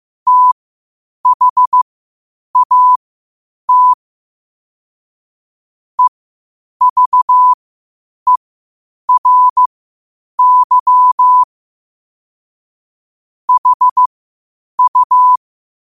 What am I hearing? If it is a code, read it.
Morse code, heard as a keyed tone, about 15 wpm, spaced as for 8 wpm, 1000 Hz. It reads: THAT EVERY HU
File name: qotdfiles / 8wpm